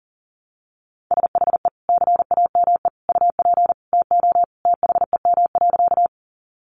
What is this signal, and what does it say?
Morse code, heard as a keyed tone, about 40 wpm, 700 Hz. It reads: SHE CAME UP TO THEM.
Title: Sentences from Top 200 Words 40wpm